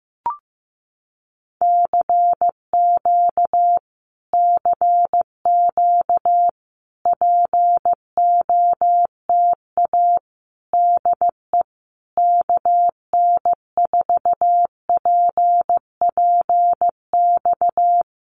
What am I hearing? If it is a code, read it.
Morse code, heard as a keyed tone, about 15 wpm, 700 Hz. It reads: CQ CQ POTA DE KN4PPX
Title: Common QSO Phrases Encore 15wpm